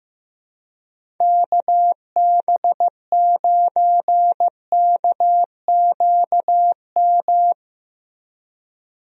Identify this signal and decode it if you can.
Morse code, heard as a keyed tone, about 15 wpm, 700 Hz. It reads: KB9KQM
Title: USA Call Signs - 6 Characters - Repeated 2x - 15wpm